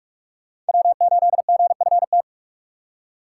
Morse code, heard as a keyed tone, about 57 wpm, 700 Hz. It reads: W8GFT